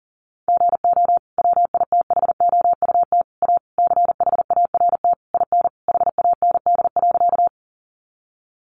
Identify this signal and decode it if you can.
Morse code, heard as a keyed tone, about 40 wpm, 700 Hz. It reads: GO WITHOUT A CHART IN HAND.
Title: Sentences from Top 1300 Words 40wpm